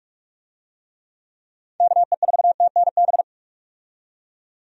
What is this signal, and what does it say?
Morse code, heard as a keyed tone, about 42 wpm, 700 Hz. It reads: KE4TNB
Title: USA Call Signs - 6 Characters - 45wpm